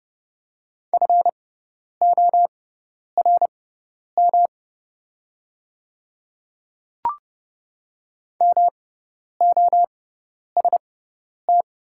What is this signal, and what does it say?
Morse code, heard as a keyed tone, about 30 wpm, spaced as for 10 wpm, 690 Hz. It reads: FORM MOST
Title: 6x Character Spacing - Top 200 Words - 30wpm